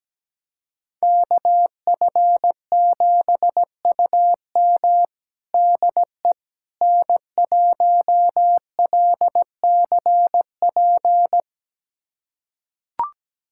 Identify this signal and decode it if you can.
Morse code, heard as a keyed tone, about 17 wpm, 700 Hz. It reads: KF7UM DE N1LCP
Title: USA Callsign DE Callsign - 5 Characters - 17wpm